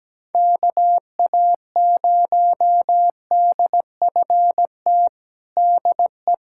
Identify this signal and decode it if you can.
Morse code, heard as a keyed tone, about 17 wpm, 700 Hz. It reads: KA0DFT DE
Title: USA Callsign DE Callsign - 6 Characters - 17wpm